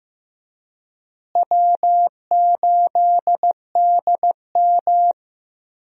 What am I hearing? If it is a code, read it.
Morse code, heard as a keyed tone, about 15 wpm, 700 Hz. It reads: W8DM
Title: USA Call Signs - 4 Characters - Repeated 3x - 15wpm